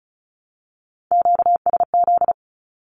Morse code, heard as a keyed tone, about 34 wpm, 700 Hz. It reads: QSZ
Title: Q-Codes Repeated 3x 35wpm